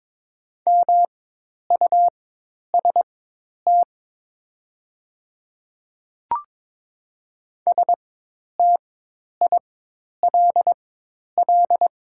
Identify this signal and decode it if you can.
Morse code, heard as a keyed tone, about 22 wpm, spaced as for 10 wpm, 700 Hz. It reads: MUST STILL
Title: 4x Character Spacing - Top 500 Words - 22wpm